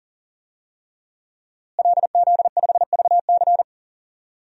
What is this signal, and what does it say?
Morse code, heard as a keyed tone, about 40 wpm, 700 Hz. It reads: LZ5VC
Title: DXCC-100 Call Signs - 5 Characters - Repeated 3x - 40wpm